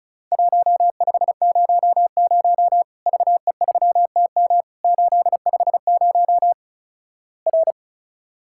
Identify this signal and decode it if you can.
Morse code, heard as a keyed tone, about 36 wpm, 700 Hz. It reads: 1500 VE3TM 850 R
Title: Worked All Europe DX Contest - QTC Exchange - 35wpm